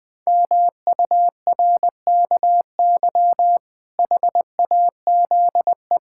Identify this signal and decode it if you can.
Morse code, heard as a keyed tone, about 21 wpm, 700 Hz. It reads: MURKY HAZE
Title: Pangram Challenge v2 - 20wpm